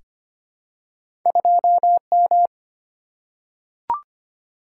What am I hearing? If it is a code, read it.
Morse code, heard as a keyed tone, about 25 wpm, 700 Hz. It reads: 2M
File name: Two Letter-Number Combinations - 25wpm